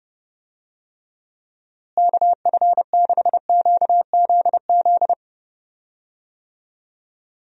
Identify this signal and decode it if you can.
Morse code, heard as a keyed tone, about 31 wpm, 700 Hz. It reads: KF6QZZ
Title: USA Call Signs - 6 Characters - Repeated 2x - 30wpm